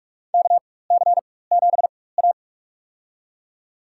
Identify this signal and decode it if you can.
Morse code, heard as a keyed tone, about 46 wpm, spaced as for 21 wpm, 700 Hz. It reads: KC7A
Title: USA Call Signs - 4 Characters - 4x Character Spacing - 45wpm